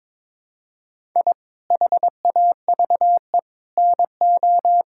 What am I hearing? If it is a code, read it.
Morse code, heard as a keyed tone, about 22 wpm, 700 Hz. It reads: I HAVE NO
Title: Sentences from Top 1000 Words - Review Part 6 - 22wpm